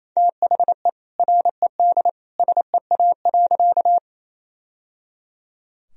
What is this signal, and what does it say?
Morse code, heard as a keyed tone, about 28 wpm, 700 Hz. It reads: THE RED SEA.
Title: Sentences from Top 1000 Words - Review Part 2 - 28wpm